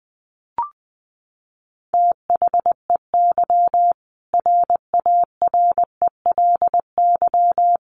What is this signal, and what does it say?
Morse code, heard as a keyed tone, about 20 wpm, 700 Hz. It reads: THEY RARELY
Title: Sentences from Top 1400 Words 20wpm